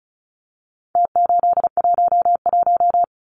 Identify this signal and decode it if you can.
Morse code, heard as a keyed tone, about 34 wpm, 700 Hz. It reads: T811
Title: Numbers - Contest Style - 4-Digit 35wpm